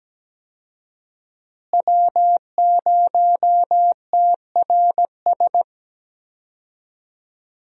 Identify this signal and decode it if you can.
Morse code, heard as a keyed tone, about 17 wpm, 700 Hz. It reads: W0TRS
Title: USA Call Signs - 5 Characters - Repeated 2x - 17wpm